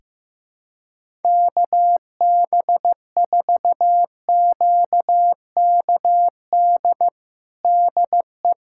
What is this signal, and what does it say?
Morse code, heard as a keyed tone, about 15 wpm, 700 Hz. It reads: KB4QKD DE
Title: USA Callsign DE Callsign - 6 Characters - 15wpm